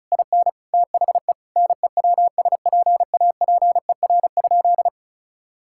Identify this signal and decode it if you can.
Morse code, heard as a keyed tone, about 36 wpm, 700 Hz. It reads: IN THE NEWSPAPER?
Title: Sentences from Top 1000 Words - Review Part 6 - 35wpm